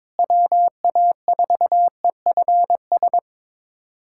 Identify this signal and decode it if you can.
Morse code, heard as a keyed tone, about 22 wpm, 700 Hz. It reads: WA4EFS
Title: USA Callsign DE Callsign - 6 Characters - 22wpm